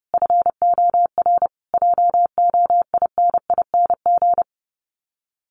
Morse code, heard as a keyed tone, about 30 wpm, 700 Hz. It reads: FOR JOINING
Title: Sets of 5 Words Encore2 30wpm